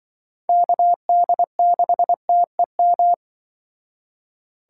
Morse code, heard as a keyed tone, about 24 wpm, 700 Hz. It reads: KD6TEM
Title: Speed Racing - USA 2x3 Call Signs 20wpm